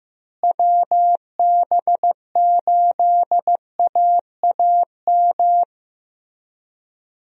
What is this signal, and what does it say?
Morse code, heard as a keyed tone, about 16 wpm, 700 Hz. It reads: WB8AAM
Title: USA Call Signs - 6 Characters - Repeated 2x - 15wpm